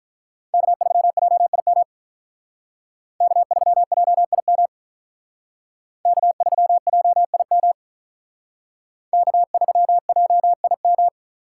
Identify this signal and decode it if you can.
Morse code, heard as a keyed tone, about 46 wpm, 700 Hz. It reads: K3JIM K3JIM K3JIM K3JIM